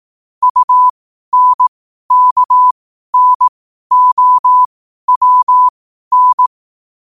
Morse code, heard as a keyed tone, about 18 wpm, spaced as for 12 wpm, 1000 Hz. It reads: UNKNOWN
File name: qotdfiles / 12wpm